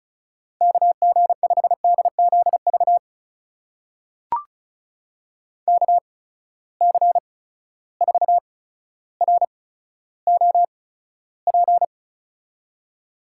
Morse code, heard as a keyed tone, about 36 wpm, spaced as for 15 wpm, 700 Hz. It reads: KG5DZV KC4ROP
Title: USA Call Signs - 6 Characters - 8x Character Spacing - 35wpm